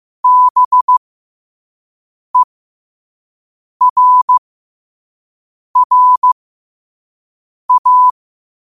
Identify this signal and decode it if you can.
Morse code, heard as a keyed tone, about 15 wpm, spaced as for 5 wpm, 1000 Hz. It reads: BERRA
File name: qotdfiles / 5wpm